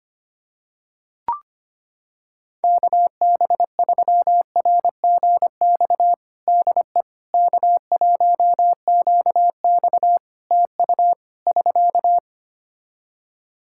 Morse code, heard as a keyed tone, about 25 wpm, 700 Hz. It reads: KB3RGX DE K1QX TU <SK>
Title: Practice QSO - Exchange 4 - Ending - 25wpm